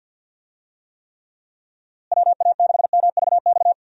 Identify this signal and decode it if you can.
Morse code, heard as a keyed tone, about 51 wpm, 700 Hz. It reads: WA6MVX
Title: USA Call Signs - 6 Characters - Repeated 2x - 50wpm